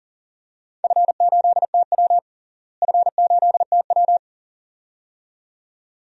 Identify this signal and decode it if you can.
Morse code, heard as a keyed tone, about 40 wpm, 700 Hz. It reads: F8TW F8TW
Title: DXCC-100 Call Signs - 4 Characters - Repeated 2x - 40wpm